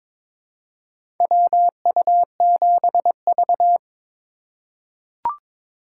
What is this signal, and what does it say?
Morse code, heard as a keyed tone, about 22 wpm, 700 Hz. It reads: WU7V
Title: USA Call Signs - 4 Characters - 3x Character Spacing - 22wpm